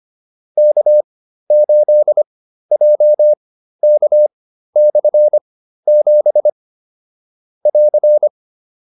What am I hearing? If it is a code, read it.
Morse code, heard as a keyed tone, about 25 wpm, spaced as for 13 wpm, 600 Hz. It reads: K8JK/7 <AR>
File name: QSO310 13